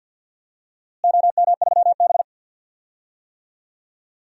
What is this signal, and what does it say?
Morse code, heard as a keyed tone, about 46 wpm, 700 Hz. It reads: OM3B
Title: DXCC-100 Call Signs - 4 Characters - Repeated 2x - 50wpm